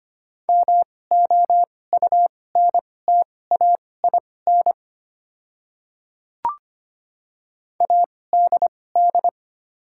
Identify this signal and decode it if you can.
Morse code, heard as a keyed tone, about 25 wpm, spaced as for 18 wpm, 690 Hz. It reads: MOUNTAIN ADD
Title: 2x Character Spacing - Top 300 Words - 25wpm